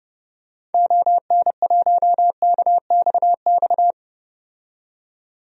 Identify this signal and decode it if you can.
Morse code, heard as a keyed tone, about 30 wpm, 700 Hz. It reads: ON1KXX
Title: DXCC-100 Call Signs - 6 Characters - 30wpm